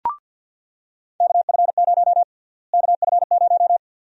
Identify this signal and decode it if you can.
Morse code, heard as a keyed tone, about 47 wpm, 700 Hz. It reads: KF0 KF0
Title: USA Call Sign Prefixes - Repeated 2x - 50wpm